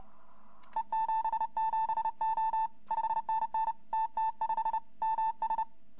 Morse code, heard as a keyed tone, about 30 wpm, 890 Hz. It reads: E77O 5NN TT5 MS
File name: hfkup2018 / greske / yu2v